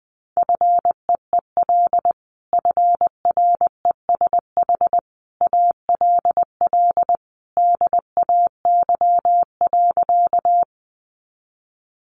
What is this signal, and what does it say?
Morse code, heard as a keyed tone, about 20 wpm, 700 Hz. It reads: FEEL FRESH ALL DAY.
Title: Sentences from Top 1200 Words 20wpm